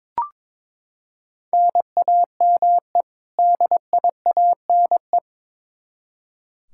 Morse code, heard as a keyed tone, about 22 wpm, 700 Hz. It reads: NAME DIANE
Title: K1USN SST QSO Practice 22wpm